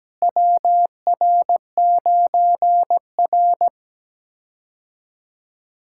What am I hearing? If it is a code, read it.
Morse code, heard as a keyed tone, about 17 wpm, 700 Hz. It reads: WR9R
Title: K1USN SST QSO Practice - 17wpm